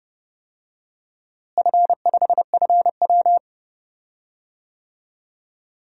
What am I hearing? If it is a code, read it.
Morse code, heard as a keyed tone, about 32 wpm, 700 Hz. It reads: F5FW
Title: DXCC-100 Call Signs - 4 Characters - Repeated 3x - 30wpm